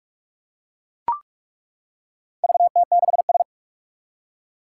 Morse code, heard as a keyed tone, about 42 wpm, 700 Hz. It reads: VT6S
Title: Four Challenging Letter-Number Combinations 45wpm